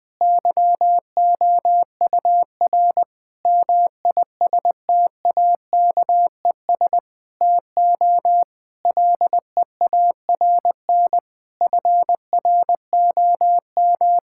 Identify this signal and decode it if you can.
Morse code, heard as a keyed tone, about 20 wpm, 700 Hz. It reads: YOUR MISTAKES TO LEARN FROM
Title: Difficult Sentences with Hints - 20wpm